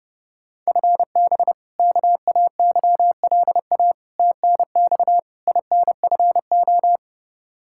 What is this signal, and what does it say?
Morse code, heard as a keyed tone, about 30 wpm, 700 Hz. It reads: FB KAYLA TNX INFO
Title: K1USN SST QSO Practice 30wpm